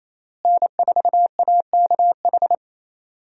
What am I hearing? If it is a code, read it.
Morse code, heard as a keyed tone, about 28 wpm, 700 Hz. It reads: N4AKH
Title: USA Call Signs - 5 Characters - Repeated 3x - 28wpm